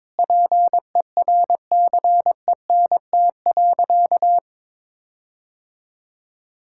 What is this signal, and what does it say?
Morse code, heard as a keyed tone, about 22 wpm, 700 Hz. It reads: PERCENT.
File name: practice / 22wpm / Sentences with 8-Word Phrases - 22wpm